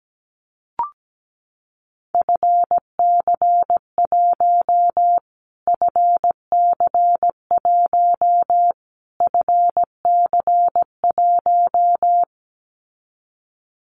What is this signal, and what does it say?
Morse code, heard as a keyed tone, about 17 wpm, 700 Hz. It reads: FC1 FC1 FC1
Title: Callsign Prefixes for Top 25 DXCC - Repeated 3x - 17wpm